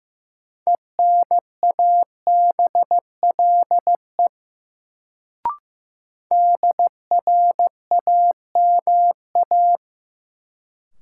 Morse code, heard as a keyed tone, about 16 wpm, 700 Hz. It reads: ENABLE DRAMA
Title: Sentences from Top 1500 Words - New Word Drill - 15wpm